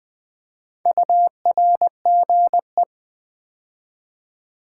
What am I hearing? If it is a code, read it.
Morse code, heard as a keyed tone, about 20 wpm, 700 Hz. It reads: URGE
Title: Build A Word - 2 Letters At A Time - 4-Letter Words - 20wpm